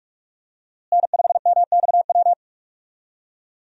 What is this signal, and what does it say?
Morse code, heard as a keyed tone, about 43 wpm, 700 Hz. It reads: N5MXW
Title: USA Call Signs - 5 Characters - Repeated 3x - 45wpm